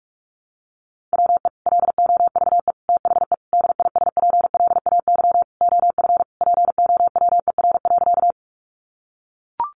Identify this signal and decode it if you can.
Morse code, heard as a keyed tone, about 44 wpm, 700 Hz. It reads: WE LOVE THE DISPLAY OF POWER.
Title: Sentences from Top 1200 Words 45wpm